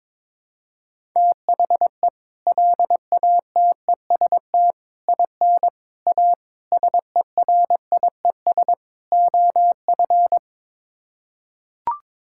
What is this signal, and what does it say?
Morse code, heard as a keyed tone, about 22 wpm, 700 Hz. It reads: THE LATEST IN A SERIES OF
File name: practice / 22wpm / Sets of 6 Words - 22wpm